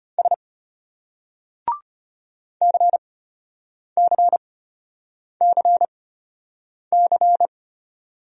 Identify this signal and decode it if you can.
Morse code, heard as a keyed tone, about 34 wpm, spaced as for 8 wpm, 700 Hz. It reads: S CCCC